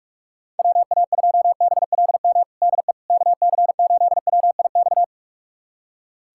Morse code, heard as a keyed tone, about 43 wpm, 700 Hz. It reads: WA2BLM DE KC8WIX
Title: USA Callsign DE Callsign 45wpm